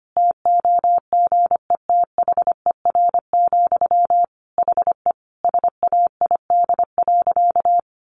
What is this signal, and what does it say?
Morse code, heard as a keyed tone, about 25 wpm, 700 Hz. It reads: TOGETHER, HE SAID.
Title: Sentences from Top 1100 Words 25wpm